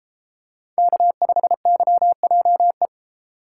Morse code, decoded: K5YJE

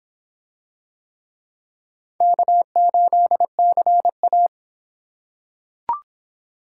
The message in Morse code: K8CA

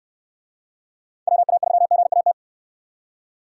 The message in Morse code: WA3GAT

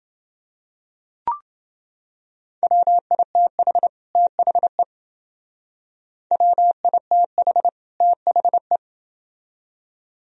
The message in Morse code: WITH THE WITH THE

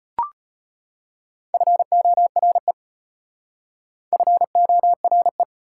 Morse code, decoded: FORE FORE

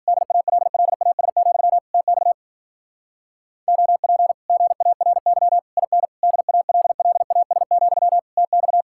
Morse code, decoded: DALLAS, TX OP GARY IN DALLAS, TX

